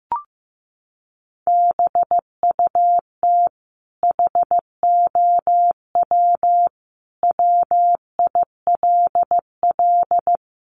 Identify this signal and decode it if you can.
Morse code, heard as a keyed tone, about 15 wpm, 700 Hz. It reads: BUT HOW WILL
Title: Sentences from Top 1400 Words 15wpm